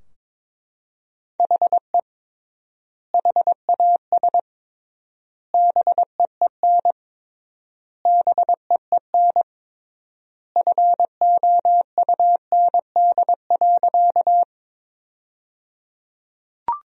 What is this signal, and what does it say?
Morse code, heard as a keyed tone, about 22 wpm, 700 Hz. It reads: HE HAS BEEN BEEN FOUND.
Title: 3x Word-Spacing - Sentences from Top 300 Words - 22wpm